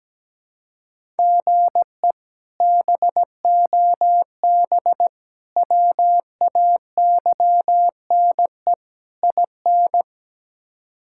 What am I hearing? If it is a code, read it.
Morse code, heard as a keyed tone, about 17 wpm, 700 Hz. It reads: GE BOB WAYNE IN
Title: K1USN SST QSO Practice - 17wpm